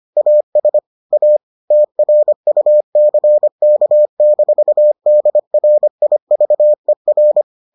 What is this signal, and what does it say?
Morse code, heard as a keyed tone, about 25 wpm, 600 Hz. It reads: AS A TRUCK-DRIVER